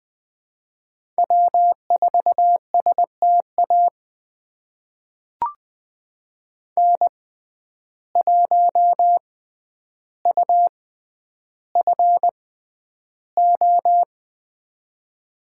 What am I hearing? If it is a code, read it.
Morse code, heard as a keyed tone, about 21 wpm, spaced as for 10 wpm, 700 Hz. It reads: W4STA N1UFO
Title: USA Call Signs - 5 Characters - 6x Character Spacing - 20wpm